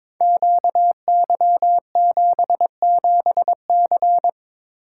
Morse code, heard as a keyed tone, about 22 wpm, 700 Hz. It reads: QY77C
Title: Code Groups - 5 Letter:Numbers - 22wpm